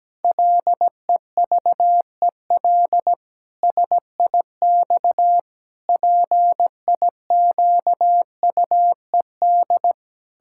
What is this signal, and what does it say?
Morse code, heard as a keyed tone, about 17 wpm, 700 Hz. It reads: LEVEL SIX PIQUED
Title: Pangram Challenge v2 - 17wpm